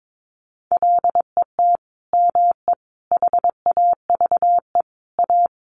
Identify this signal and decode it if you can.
Morse code, heard as a keyed tone, about 22 wpm, 700 Hz. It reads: LET ME HAVE A
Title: Sentences from Top 900 Words 22wpm